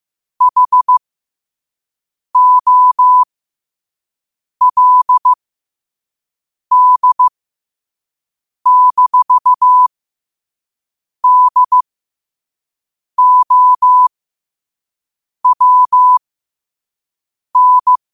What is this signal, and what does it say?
Morse code, heard as a keyed tone, about 15 wpm, spaced as for 5 wpm, 1000 Hz. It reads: HOLD-DOWN
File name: qotdfiles / 5wpm